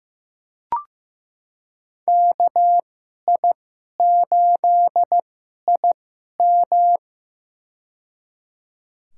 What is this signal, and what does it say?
Morse code, heard as a keyed tone, about 15 wpm, spaced as for 11 wpm, 700 Hz. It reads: KI8IM
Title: USA Call Signs - 5 Characters - 2x Character Spacing - 15wpm